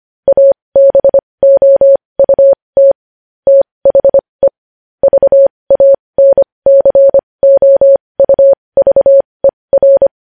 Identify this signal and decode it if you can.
Morse code, heard as a keyed tone, about 25 wpm, spaced as for 20 wpm, 550 Hz. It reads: ABOUT THE VANCOUVER
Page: news headlines in morse code at 20 wpm